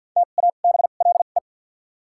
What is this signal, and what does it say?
Morse code, heard as a keyed tone, about 50 wpm, 700 Hz. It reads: T A B L E
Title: 2x Character Spacing - Top 500 Words - 50wpm